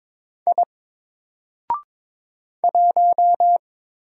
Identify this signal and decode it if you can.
Morse code, heard as a keyed tone, about 22 wpm, 700 Hz. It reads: I 1
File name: Single-Letter-Number-22wpm